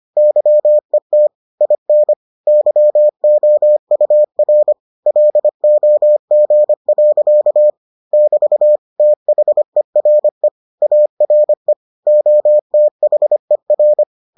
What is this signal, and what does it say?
Morse code, heard as a keyed tone, about 25 wpm, 600 Hz. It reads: YET IN YOUR LOG. <BT> THERE ARE OTHER